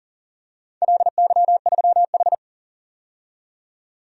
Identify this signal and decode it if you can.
Morse code, heard as a keyed tone, about 38 wpm, 700 Hz. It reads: LY3H
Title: DXCC-100 Call Signs - 4 Characters - 40wpm